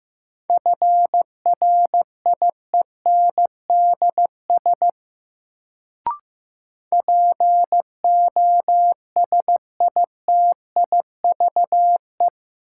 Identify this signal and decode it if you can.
Morse code, heard as a keyed tone, about 15 wpm, 700 Hz. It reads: FRIENDS POSITIVE
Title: Sentences from Top 1000 Words - Review Part 5 - New Word Drill - 15wpm